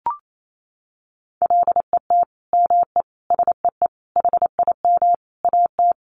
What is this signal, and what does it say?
Morse code, heard as a keyed tone, about 28 wpm, 700 Hz. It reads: LET ME SEE HIM AT
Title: Sentences from Top 300 Words 28wpm